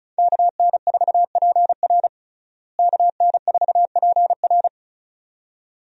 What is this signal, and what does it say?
Morse code, heard as a keyed tone, about 34 wpm, 700 Hz. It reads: KN4PR KN4PR
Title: USA Call Signs - 5 Characters - Repeated 2x - 35wpm